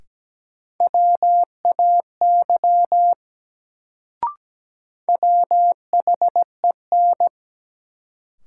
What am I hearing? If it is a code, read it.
Morse code, heard as a keyed tone, about 17 wpm, 700 Hz. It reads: WAY WHEN
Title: Sentences from Top 100 Words - New Word Drill - 17wpm